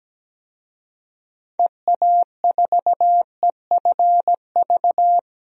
Morse code, heard as a keyed tone, about 17 wpm, 700 Hz. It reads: EA4EFV
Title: DXCC-100 Call Signs - 6 Characters - 17wpm